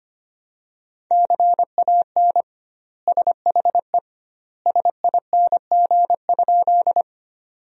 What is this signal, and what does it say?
Morse code, heard as a keyed tone, about 25 wpm, 700 Hz. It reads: CAN SHE SING?
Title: 2x Word-Spacing - Sentences from Top 400 Words - 25wpm